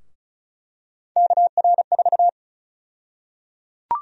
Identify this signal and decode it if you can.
Morse code, heard as a keyed tone, about 34 wpm, 700 Hz. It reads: KR4
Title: USA Call Sign Prefixes - 35wpm